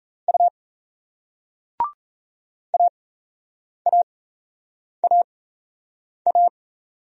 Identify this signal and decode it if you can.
Morse code, heard as a keyed tone, about 38 wpm, spaced as for 8 wpm, 700 Hz. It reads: U AAAA